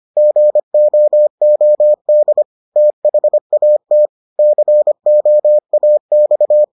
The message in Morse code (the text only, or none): GOOD THAT COAX